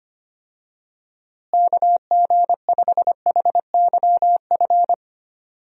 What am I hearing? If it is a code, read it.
Morse code, heard as a keyed tone, about 25 wpm, 700 Hz. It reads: KG5HYF
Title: USA Call Signs - 6 Characters - Repeated 3x - 25wpm